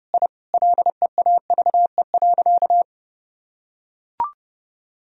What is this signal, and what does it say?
Morse code, heard as a keyed tone, about 30 wpm, 700 Hz. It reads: I LEAVE.
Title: Sentences from Top 1000 Words - Review Part 2 - 30wpm